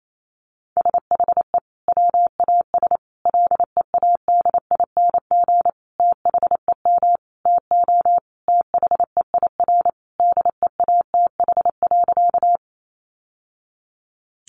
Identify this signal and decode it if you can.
Morse code, heard as a keyed tone, about 28 wpm, 700 Hz. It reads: SHE WAS LEADING THEM TO THEIR DEATH.
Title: Sentences from Top 1100 Words 28wpm